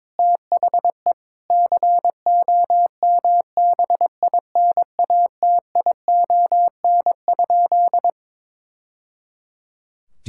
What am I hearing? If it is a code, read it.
Morse code, heard as a keyed tone, about 21 wpm, 700 Hz. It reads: THE COMBINATION?
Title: Sentences from Top 1000 Words - Review Part 6 - 22wpm